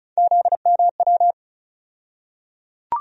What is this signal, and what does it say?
Morse code, heard as a keyed tone, about 36 wpm, 700 Hz. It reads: ZMW
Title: USA Call Sign Postfixes - 3 Letters 35wpm